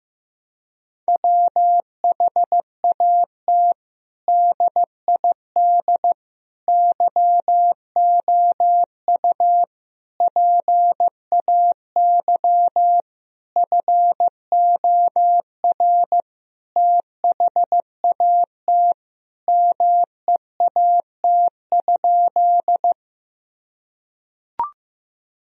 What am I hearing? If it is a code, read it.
Morse code, heard as a keyed tone, about 16 wpm, 700 Hz. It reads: WHAT DID YOU PAY FOR THAT MEAT?
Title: Sentences from Top 1000 Words - Review Part 3 - 15wpm